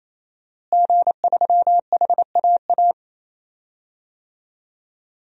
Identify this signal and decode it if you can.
Morse code, heard as a keyed tone, about 28 wpm, 700 Hz. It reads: G3HAA
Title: DXCC-100 Call Signs - 5 Characters - Repeated 3x - 28wpm